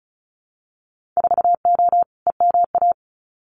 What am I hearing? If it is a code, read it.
Morse code, heard as a keyed tone, about 34 wpm, 700 Hz. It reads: 4O EMA
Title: Winter Field Day Exchanges 35wpm